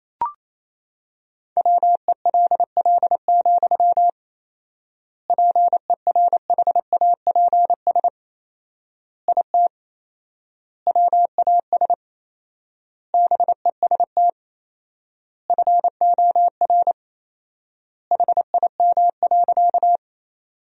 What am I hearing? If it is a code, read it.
Morse code, heard as a keyed tone, about 28 wpm, 700 Hz. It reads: WELL, PERHAPS IT WAS BEST FOR HIM.
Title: 4x Word-Spacing - Sentences from Top 500 Words - 28wpm